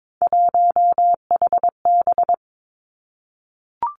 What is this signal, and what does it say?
Morse code, heard as a keyed tone, about 22 wpm, 700 Hz. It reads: 1HB
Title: Three Challenging Letter-Number Combinations 22wpm